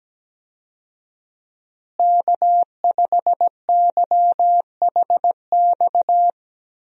Morse code, heard as a keyed tone, about 17 wpm, 700 Hz. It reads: K5YHX
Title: USA Call Signs - 5 Characters - Repeated 3x - 17wpm